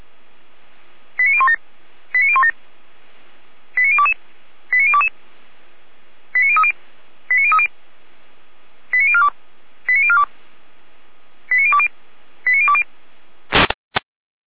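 Five-tone selective call (ZVEI), heard as quick runs of five tones, tones of about 70 ms. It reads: 89017 89017 89022 89022 89030 89030 89042 89042 89020 89020
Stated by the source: ZVEI five-tone selective call 89012, received by SDR.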